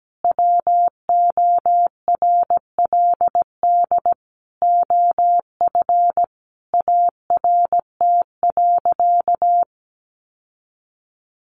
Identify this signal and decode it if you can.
Morse code, heard as a keyed tone, about 17 wpm, 700 Hz. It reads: WORLD OF ART.
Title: Sentences from Top 1500 Words 17wpm